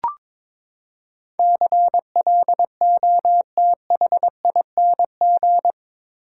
Morse code, heard as a keyed tone, about 22 wpm, 700 Hz. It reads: CLOTHING
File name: practice / 22wpm / Eight Letter Words - 22wpm